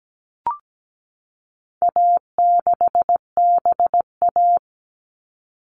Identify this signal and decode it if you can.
Morse code, heard as a keyed tone, about 17 wpm, 700 Hz. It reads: A6BA
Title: All Cut Numbers - 4-Digit 17wpm